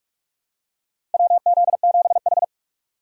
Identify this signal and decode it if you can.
Morse code, heard as a keyed tone, about 42 wpm, 700 Hz. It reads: WZ7H
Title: USA Call Signs - 4x Character Spacing - 45wpm